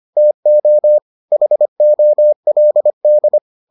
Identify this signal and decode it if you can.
Morse code, heard as a keyed tone, about 25 wpm, 600 Hz. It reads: TO HOLD